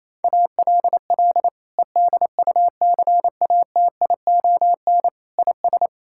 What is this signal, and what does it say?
Morse code, heard as a keyed tone, about 28 wpm, 700 Hz. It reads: ALL EDUCATION IS
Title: Sentences from Top 1000 Words - Review Part 6 - 28wpm